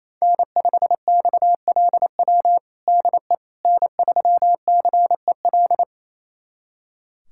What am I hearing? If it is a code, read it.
Morse code, heard as a keyed tone, about 27 wpm, 700 Hz. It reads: N5XLW DE N3CEL
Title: USA Callsign DE Callsign 28wpm